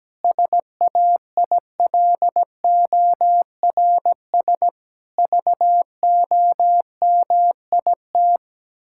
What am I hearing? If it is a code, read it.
Morse code, heard as a keyed tone, about 17 wpm, 700 Hz. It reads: SAILORS VOMIT
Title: Pangram Challenge v2 - 17wpm